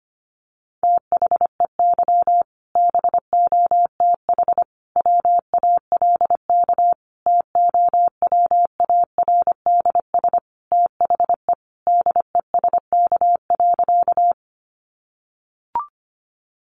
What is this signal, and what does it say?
Morse code, heard as a keyed tone, about 25 wpm, 700 Hz. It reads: THEY BOTH WALK TOWARDS THE DESK.
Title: Additional Easy Phrases 25wpm